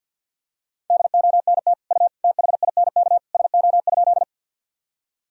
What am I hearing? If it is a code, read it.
Morse code, heard as a keyed tone, about 52 wpm, 700 Hz. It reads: DONT U THINK SO?
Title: Sentences from Top 300 Words and Abbreviations 50wpm